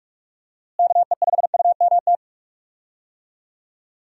Morse code, heard as a keyed tone, about 46 wpm, 700 Hz. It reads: KE5UMT
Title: USA Call Signs - 6 Characters - Repeated 2x - 45wpm